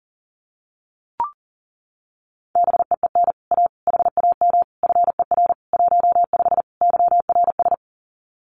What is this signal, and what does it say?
Morse code, heard as a keyed tone, about 41 wpm, 700 Hz. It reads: BEEN A HAM FER 15 YRS
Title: Common QSO Phrases Encore 40wpm